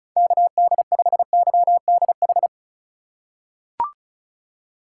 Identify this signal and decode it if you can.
Morse code, heard as a keyed tone, about 34 wpm, 700 Hz. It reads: KD5YDH
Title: USA Call Signs - 6 Characters - 2x Character Spacing - 35wpm